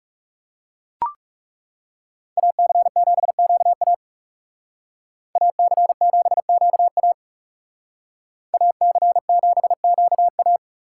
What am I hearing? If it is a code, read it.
Morse code, heard as a keyed tone, about 38 wpm, 700 Hz. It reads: AC7QA AC7QA AC7QA